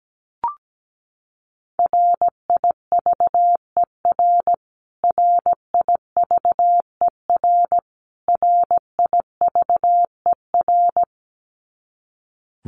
Morse code, heard as a keyed tone, about 17 wpm, 700 Hz. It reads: RIVER RIVER RIVER
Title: Top 500 Words Repeated 17wpm